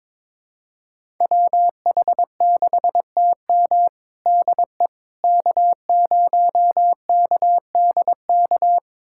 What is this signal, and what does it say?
Morse code, heard as a keyed tone, about 21 wpm, 700 Hz. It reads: WH6TM DE K0KDK
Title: USA Callsign DE Callsign - 5 Characters - 22wpm